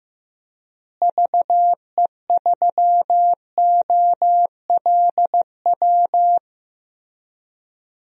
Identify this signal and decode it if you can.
Morse code, heard as a keyed tone, about 15 wpm, 700 Hz. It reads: VE3OLW
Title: DXCC-100 Call Signs - 6 Characters - Repeated 2x - 15wpm